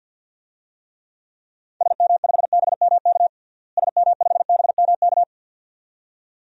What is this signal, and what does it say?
Morse code, heard as a keyed tone, about 48 wpm, 700 Hz. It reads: SM5BMK SM5BMK
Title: DXCC-100 Call Signs - 6 Characters - Repeated 2x - 50wpm